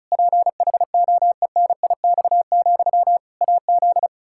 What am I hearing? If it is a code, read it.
Morse code, heard as a keyed tone, about 36 wpm, 700 Hz. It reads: PHOENIX, AZ
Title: Practice QSO - Exchange 1 - Beginning - 35wpm